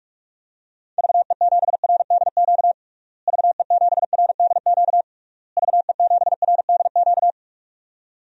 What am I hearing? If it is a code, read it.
Morse code, heard as a keyed tone, about 47 wpm, 700 Hz. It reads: VE7RDQ VE7RDQ VE7RDQ
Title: DXCC-100 Call Signs - 6 Characters - Repeated 3x - 45wpm